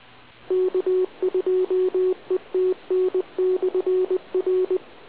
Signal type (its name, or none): Morse code